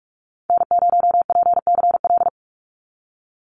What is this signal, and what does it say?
Morse code, heard as a keyed tone, about 44 wpm, 700 Hz. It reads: N0PCL
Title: NA SOTA Activator Callsigns 45wpm